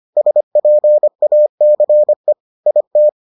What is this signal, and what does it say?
Morse code, heard as a keyed tone, about 25 wpm, 600 Hz. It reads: SPACE IT